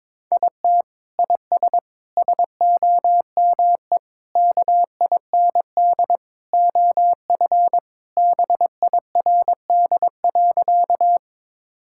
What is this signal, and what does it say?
Morse code, heard as a keyed tone, about 22 wpm, 700 Hz. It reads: IT IS SOME KIND OF BIRD.
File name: practice / 22wpm / Sentences with 3-Word Phrases - 22wpm